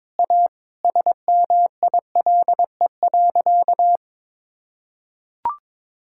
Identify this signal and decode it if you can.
Morse code, heard as a keyed tone, about 21 wpm, 700 Hz. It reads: A SMILE.
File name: practice / 22wpm / Sentences with 2-Word Phrases - 22wpm